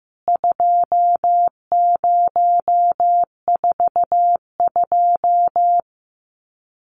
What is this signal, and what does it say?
Morse code, heard as a keyed tone, about 15 wpm, 700 Hz. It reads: 2042
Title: CWT Exchange 15wpm